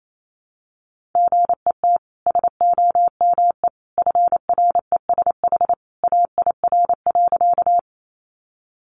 Morse code, heard as a keyed tone, about 28 wpm, 700 Hz. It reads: GET SOME FRESH AIR.
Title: Sentences from Top 1200 Words 28wpm